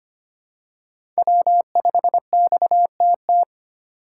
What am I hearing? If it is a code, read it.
Morse code, heard as a keyed tone, about 25 wpm, 700 Hz. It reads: W5XTT
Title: USA Call Signs - 3x Character Spacing - 25wpm